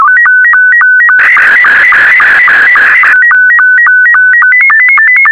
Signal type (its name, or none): SSTV (slow-scan television)